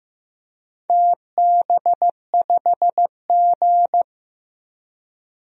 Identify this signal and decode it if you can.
Morse code, heard as a keyed tone, about 15 wpm, 700 Hz. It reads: TB5G
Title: DXCC-100 Call Signs - 4 Characters - 15wpm